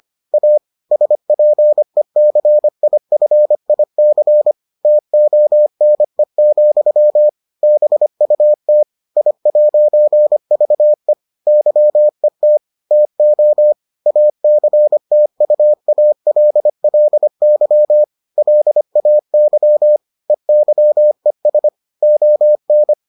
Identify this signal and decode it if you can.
Morse code, heard as a keyed tone, about 25 wpm, 600 Hz. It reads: A SPECIFIC TONE, BUT I'VE YET TO ACTUALLY LAY EYES ON